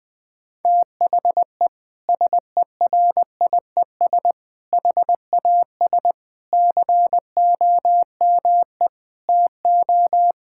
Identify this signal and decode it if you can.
Morse code, heard as a keyed tone, about 20 wpm, 700 Hz. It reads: THE SERIES HAS COME TO
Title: Sentences from Top 1000 Words - Review Part 5 - 20wpm